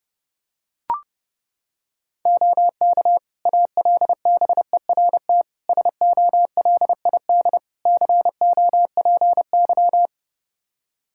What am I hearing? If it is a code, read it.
Morse code, heard as a keyed tone, about 30 wpm, 700 Hz. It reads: OK ALBERT SOLID COPY
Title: Practice QSO - Exchange 3 - More Info - 30wpm